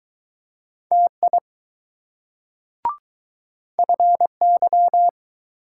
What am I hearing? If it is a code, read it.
Morse code, heard as a keyed tone, about 20 wpm, 700 Hz. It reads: TI FY